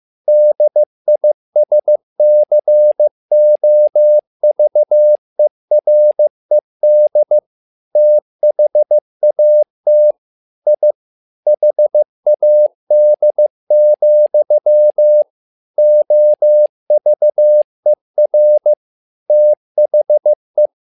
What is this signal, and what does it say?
Morse code, heard as a keyed tone, about 15 wpm, 600 Hz. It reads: DISCOVERED THAT I HAD, OVER THE
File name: learning center / tower trouble / tower trouble 15wpm